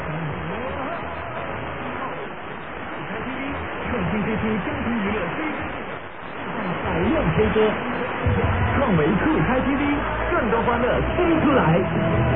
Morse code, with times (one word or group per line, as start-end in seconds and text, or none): none